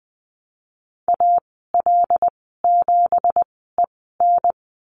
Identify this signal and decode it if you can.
Morse code, heard as a keyed tone, about 20 wpm, spaced as for 15 wpm, 700 Hz. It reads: AL7EN
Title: 193 Lesson 31g - Letter v - Call Signs 20wpm